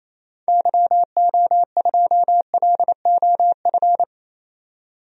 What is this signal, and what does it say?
Morse code, heard as a keyed tone, about 28 wpm, 700 Hz. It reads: YO2LOF
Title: DXCC-100 Call Signs - 6 Characters - 28wpm